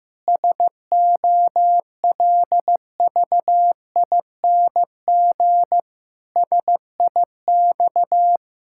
Morse code, heard as a keyed tone, about 15 wpm, 700 Hz. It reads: SOLVING SIX
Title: Pangram Challenge v2 - 15wpm